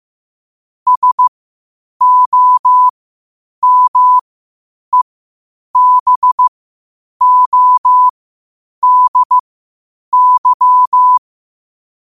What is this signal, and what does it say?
Morse code, heard as a keyed tone, about 15 wpm, spaced as for 8 wpm, 1000 Hz. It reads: SOMEBODY
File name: qotdfiles / 8wpm